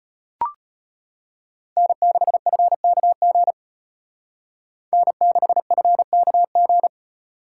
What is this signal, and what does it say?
Morse code, heard as a keyed tone, about 31 wpm, 700 Hz. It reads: N6FKG N6FKG
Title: Speed Racing - USA 1x3 Call Signs 25wpm